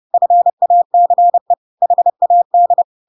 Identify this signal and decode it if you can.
Morse code, heard as a keyed tone, about 30 wpm, 700 Hz. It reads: FACE HAD
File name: Pinocchio / Pinocchio ch1 30 30 ik1hge